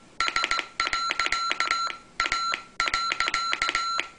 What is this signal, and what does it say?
Morse code, heard as a keyed tone, about 17 wpm, 1360 Hz, with noise. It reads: S.A.